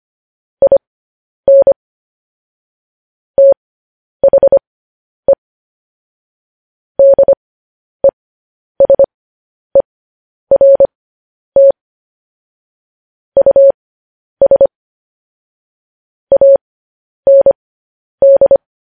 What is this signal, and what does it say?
Morse code, heard as a keyed tone, about 25 wpm, spaced as for 10 wpm, 550 Hz. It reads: IN THE DESERT US AND
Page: news-headlines-in-morse-code-at-10-wpm